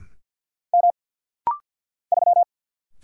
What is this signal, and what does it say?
Morse code, heard as a keyed tone, about 48 wpm, 690 Hz. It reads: M 3
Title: Single Letter-Number - Rapid-Fire 50wpm